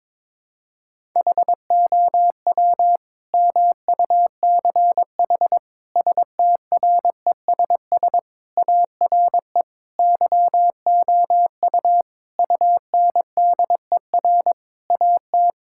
Morse code, heard as a keyed tone, about 23 wpm, 700 Hz. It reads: HOW MUCH STRESS ARE YOU UNDER AT
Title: Sentences from Top 1000 Words - Review Part 6 - 22wpm